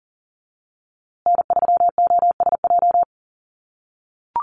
Morse code, heard as a keyed tone, about 40 wpm, 700 Hz. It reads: N3OSJ
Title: USA Call Signs Repeated 2x 40wpm